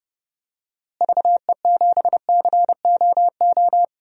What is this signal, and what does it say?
Morse code, heard as a keyed tone, about 30 wpm, 700 Hz. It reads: VE7COO
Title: DXCC-100 Call Signs - 6 Characters - Repeated 3x - 30wpm